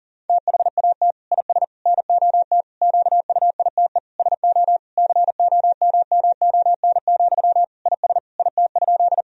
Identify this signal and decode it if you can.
Morse code, heard as a keyed tone, about 40 wpm, 700 Hz. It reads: THAT IS NOT QUITE SO COMMON, IS IT?
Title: Sentences from Top 1000 Words - Review Part 3 - 40wpm